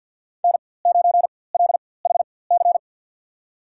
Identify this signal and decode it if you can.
Morse code, heard as a keyed tone, about 53 wpm, spaced as for 23 wpm, 700 Hz. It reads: N9LHC
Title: USA Call Signs - 5 Characters - 4x Character Spacing - 50wpm